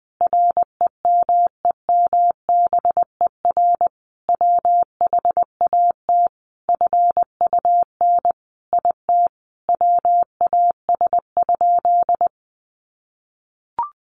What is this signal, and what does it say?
Morse code, heard as a keyed tone, about 19 wpm, 700 Hz. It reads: REMEMBER WHAT FUN IT WAS?
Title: Sentences from Top 700 Words 20wpm